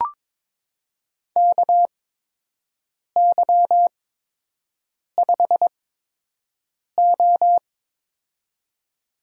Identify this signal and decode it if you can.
Morse code, heard as a keyed tone, about 22 wpm, spaced as for 6 wpm, 700 Hz. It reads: KY5O
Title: USA Call Signs - 4 Characters - 8x Character Spacing - 22wpm